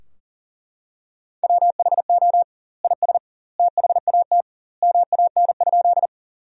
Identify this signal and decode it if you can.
Morse code, heard as a keyed tone, about 40 wpm, 700 Hz. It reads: WHO IS THAT MAN?
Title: 2x Word-Spacing - Sentences from Top 200 Words - 40wpm